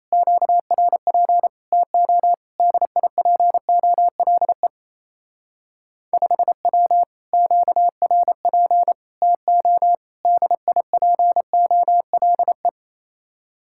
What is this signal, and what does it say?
Morse code, heard as a keyed tone, about 33 wpm, 700 Hz. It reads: QRP TO DIPOLE 5W QRP TO DIPOLE
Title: Speed-Racing Common QSO Phrases - 28wpm